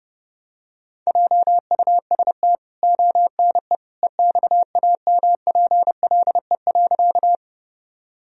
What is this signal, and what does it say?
Morse code, heard as a keyed tone, about 30 wpm, 700 Hz. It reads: JUST ONE EXAMPLE.
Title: Sentences from Top 1000 Words - Review Part 1 - 30wpm